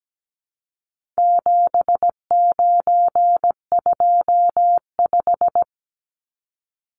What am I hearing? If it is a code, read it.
Morse code, heard as a keyed tone, about 17 wpm, 700 Hz. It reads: 7925
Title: Numbers - 4-Digit 17wpm